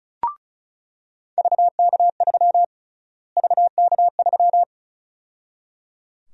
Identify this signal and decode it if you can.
Morse code, heard as a keyed tone, about 34 wpm, 700 Hz. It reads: VK3 VK3
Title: USA Call Sign Prefixes - Repeated 2x - 35wpm